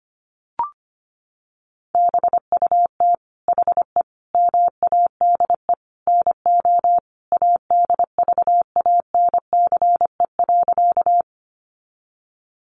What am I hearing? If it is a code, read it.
Morse code, heard as a keyed tone, about 25 wpm, 700 Hz. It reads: BUT HE MADE NO ADVANCE.
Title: Sentences from Top 1400 Words 25wpm